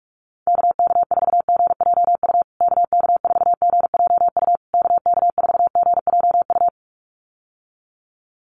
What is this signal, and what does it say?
Morse code, heard as a keyed tone, about 44 wpm, 700 Hz. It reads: KK4GJU KK4GJU KK4GJU
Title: USA Call Signs Repeated 3x 45wpm